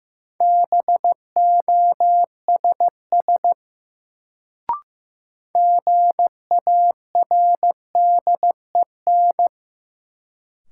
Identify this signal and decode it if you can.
Morse code, heard as a keyed tone, about 15 wpm, 700 Hz. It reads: BOSS GARDEN
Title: Sentences from Top 700 Words - New Word Drill - 15wpm